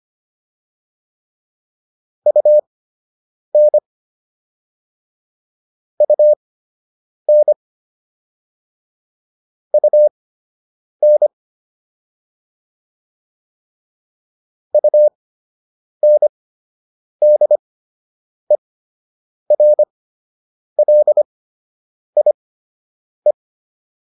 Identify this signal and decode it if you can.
Morse code, heard as a keyed tone, about 25 wpm, spaced as for 8 wpm, 600 Hz. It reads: UN UN UN UNDERLIE